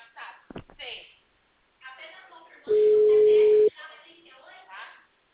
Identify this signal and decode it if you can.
Ringback tone (Europe, 425 Hz), with speech.
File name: out- -2012- - - .60741